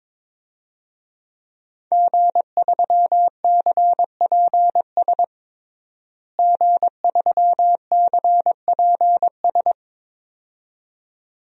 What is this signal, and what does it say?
Morse code, heard as a keyed tone, about 22 wpm, 700 Hz. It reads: G3CPS G3CPS
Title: DXCC-100 Call Signs - 5 Characters - Repeated 2x - 22wpm